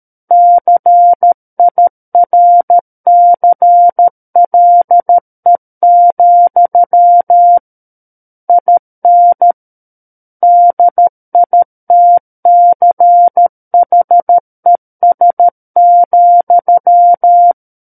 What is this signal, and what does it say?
Morse code, heard as a keyed tone, about 13 wpm, 700 Hz. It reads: CIRCLE, IN DITCHES,
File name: Book One / 04-Chapter 4 - 13 WPM